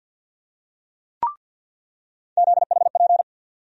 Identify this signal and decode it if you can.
Morse code, heard as a keyed tone, about 49 wpm, 700 Hz. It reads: ZHP
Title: USA Call Sign Postfixes - 3 Letters 50wpm